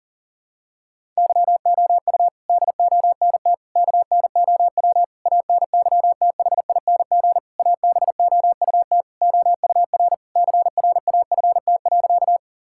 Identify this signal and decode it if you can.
Morse code, heard as a keyed tone, about 40 wpm, 700 Hz. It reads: YOU DONT KNOW ANYTHING ABOUT OUR CRAFT.
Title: Sentences from Top 1000 Words - Review Part 6 - 40wpm